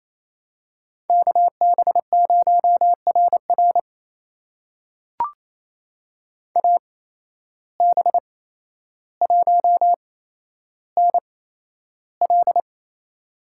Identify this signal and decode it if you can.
Morse code, heard as a keyed tone, about 28 wpm, spaced as for 12 wpm, 700 Hz. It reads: KB0RR AB1NL
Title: USA Call Signs - 5 Characters - 8x Character Spacing - 28wpm